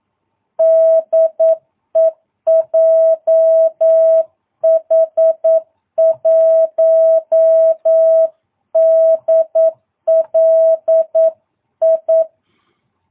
Morse code, heard as a keyed tone, about 9 wpm, 650 Hz. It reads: DE JH1DLI
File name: jh8cbh / 2022cw